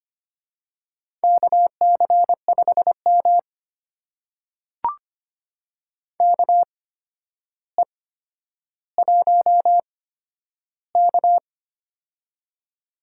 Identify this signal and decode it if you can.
Morse code, heard as a keyed tone, about 25 wpm, spaced as for 11 wpm, 700 Hz. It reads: KC5M KE1K
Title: USA Call Signs - 4 Characters - 8x Character Spacing - 25wpm